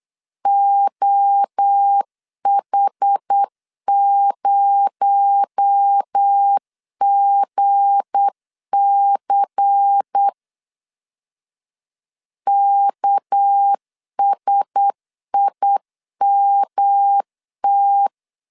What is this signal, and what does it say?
Morse code, heard as a keyed tone, about 8 wpm, 800 Hz. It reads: OH0GC KSIMT
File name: LEZIONI / 7 GF9Z / 7 GF9Z 8wpm 3213